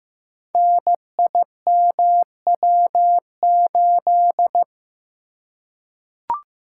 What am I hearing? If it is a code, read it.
Morse code, heard as a keyed tone, about 15 wpm, 700 Hz. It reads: NIMW8
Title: Code Groups - 5 Letter:Numbers - 15wpm